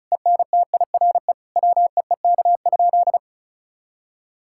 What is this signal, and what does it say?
Morse code, heard as a keyed tone, about 36 wpm, 700 Hz. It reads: ENTIRE WEEK?
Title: Sentences from Top 1000 Words - Review Part 5 - 35wpm